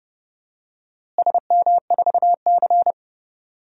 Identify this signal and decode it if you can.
Morse code, heard as a keyed tone, about 30 wpm, 700 Hz. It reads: SM4C
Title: DXCC-100 Call Signs - 4 Characters - Repeated 2x - 30wpm